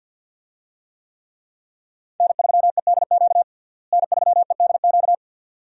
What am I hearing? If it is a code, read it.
Morse code, heard as a keyed tone, about 53 wpm, 700 Hz. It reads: N3EDQ N3EDQ
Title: USA Call Signs - 5 Characters - Repeated 2x - 50wpm